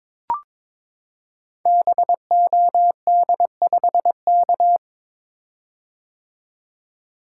Morse code, heard as a keyed tone, about 22 wpm, 700 Hz. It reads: BOD5K
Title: Code Groups - 5 Letter:Numbers - 22wpm